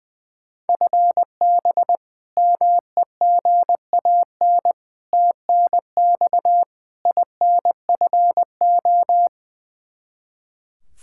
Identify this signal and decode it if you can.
Morse code, heard as a keyed tone, about 21 wpm, 700 Hz. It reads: FB MEGAN TNX INFO
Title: Practice QSO - Exchange 3 - More Info - 20wpm